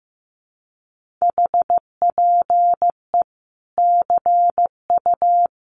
Morse code, heard as a keyed tone, about 15 wpm, 700 Hz. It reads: HPE CU
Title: Common QSO Phrases 15wpm